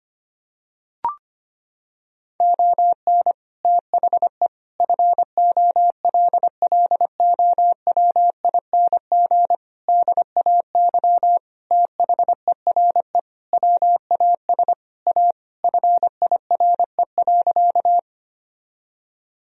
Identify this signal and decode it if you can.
Morse code, heard as a keyed tone, about 25 wpm, 700 Hz. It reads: ON THE FOLLOWING DAY THERE WAS A FIRE.
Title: Sentences from Top 1000 Words - Review Part 4 - 25wpm